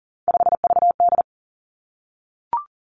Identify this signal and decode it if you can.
Morse code, heard as a keyed tone, about 40 wpm, 700 Hz. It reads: 5VD
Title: Three Challenging Letter-Number Combinations 40wpm